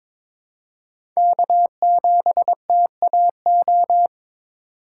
Morse code, heard as a keyed tone, about 22 wpm, 700 Hz. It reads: K7TAO
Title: USA Call Signs - 5 Characters - 3x Character Spacing - 22wpm